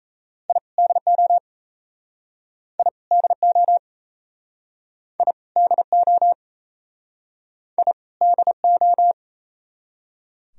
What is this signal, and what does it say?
Morse code, heard as a keyed tone, about 40 wpm, 700 Hz. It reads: I DO I DO I DO I DO